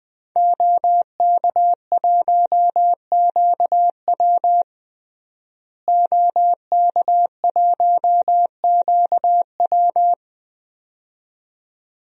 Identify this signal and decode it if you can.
Morse code, heard as a keyed tone, about 20 wpm, 700 Hz. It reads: OK1QW OK1QW
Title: DXCC-100 Call Signs - 5 Characters - Repeated 2x - 20wpm